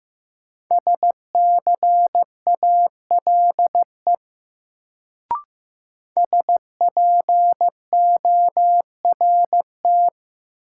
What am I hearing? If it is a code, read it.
Morse code, heard as a keyed tone, about 15 wpm, 700 Hz. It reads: SCALE SPORT
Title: Sentences from Top 1000 Words - Review Part 4 - New Word Drill - 15wpm